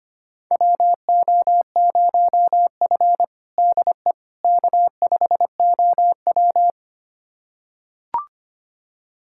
Morse code, heard as a keyed tone, about 25 wpm, 700 Hz. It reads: WO0F DE K5OW
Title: USA Callsign DE Callsign - 4 Characters - 25wpm